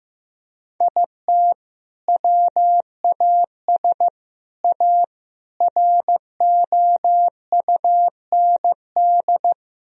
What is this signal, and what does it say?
Morse code, heard as a keyed tone, about 16 wpm, 700 Hz. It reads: IT WAS A ROUND
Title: Sentences from Top 1000 Words - Review Part 2 - 15wpm